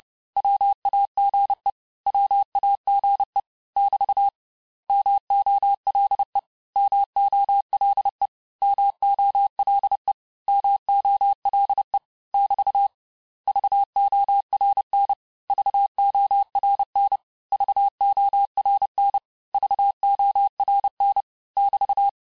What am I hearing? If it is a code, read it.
Morse code, heard as a keyed tone, about 30 wpm, 800 Hz. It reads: WAGE WAGE <BT> MOLE MOLE MOLE MOLE <BT> VORN VORN VORN VORN <BT>
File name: cours-du-mercredi / audios / cours-854-mots-vitesse-28-mpm